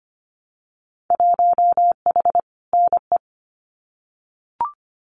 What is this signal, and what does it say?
Morse code, heard as a keyed tone, about 25 wpm, 700 Hz. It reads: 1H NE
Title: Winter Field Day Exchanges Encore 25wpm